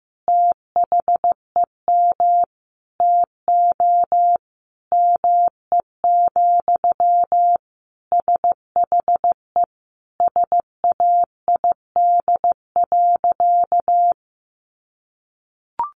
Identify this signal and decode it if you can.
Morse code, heard as a keyed tone, about 15 wpm, 700 Hz. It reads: THEM TO ME, SHE SAID.
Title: Sentences from Top 500 Words 15wpm